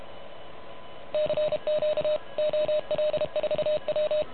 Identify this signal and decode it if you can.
Morse code, heard as a keyed tone, about 31 wpm, 590 Hz. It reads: CQ OL4W
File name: truSDX / MWC29.8